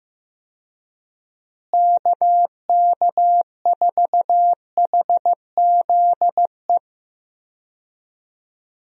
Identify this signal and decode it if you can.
Morse code, heard as a keyed tone, about 16 wpm, 700 Hz. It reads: KK4HZE
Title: USA Call Signs - 6 Characters - Repeated 2x - 15wpm